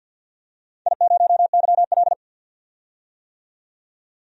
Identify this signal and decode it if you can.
Morse code, heard as a keyed tone, about 51 wpm, 700 Hz. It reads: I0YL
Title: DXCC-100 Call Signs - 4 Characters - 50wpm